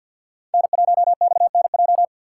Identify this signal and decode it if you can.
Morse code, heard as a keyed tone, about 53 wpm, 690 Hz. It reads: N1XNJ
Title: USA Call Signs - 4x Character Spacing - 50wpm